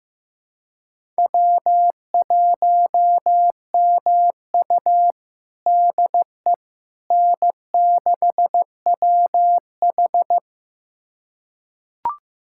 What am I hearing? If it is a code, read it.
Morse code, heard as a keyed tone, about 15 wpm, 700 Hz. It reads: W1MU DE N6WH
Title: USA Callsign DE Callsign - 4 Characters - 15wpm